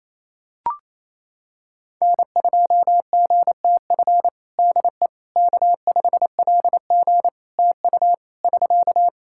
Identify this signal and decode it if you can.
Morse code, heard as a keyed tone, about 28 wpm, 700 Hz. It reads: N2GTF DE K5LG TU <SK>
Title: Practice QSO - Exchange 4 - Ending - 28wpm